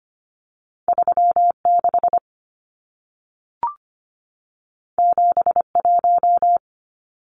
Morse code, heard as a keyed tone, about 25 wpm, 700 Hz. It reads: 36 71
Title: Numbers - 2-Digit 25wpm